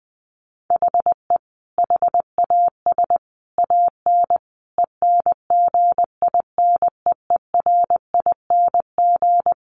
Morse code, heard as a keyed tone, about 20 wpm, 700 Hz. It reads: HE HAS AN ENGINEERING
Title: Sentences from Top 1500 Words 20wpm